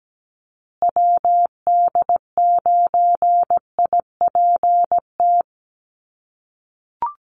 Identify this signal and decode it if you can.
Morse code, heard as a keyed tone, about 17 wpm, 700 Hz. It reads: WD9IPT
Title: USA Call Signs Repeated 3x 17wpm